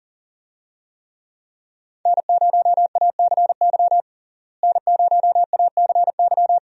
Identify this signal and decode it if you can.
Morse code, heard as a keyed tone, about 40 wpm, 700 Hz. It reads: N0ACY N0ACY
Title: USA Call Signs - 5 Characters - Repeated 2x - 40wpm